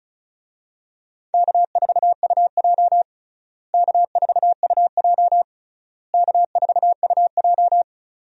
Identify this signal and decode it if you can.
Morse code, heard as a keyed tone, about 34 wpm, 700 Hz. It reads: K4UJ K4UJ K4UJ
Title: USA Call Signs - 4 Characters - Repeated 3x - 35wpm